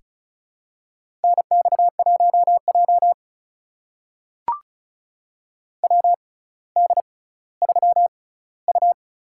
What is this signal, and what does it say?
Morse code, heard as a keyed tone, about 36 wpm, spaced as for 18 wpm, 700 Hz. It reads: NX1J WD3U
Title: USA Call Signs - 4 Characters - 6x Character Spacing - 35wpm